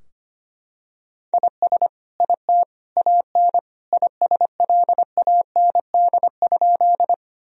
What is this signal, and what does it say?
Morse code, heard as a keyed tone, about 25 wpm, 700 Hz. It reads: IS IT AN ISLAND?
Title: Sentences from Top 1000 Words - Review Part 3 - 25wpm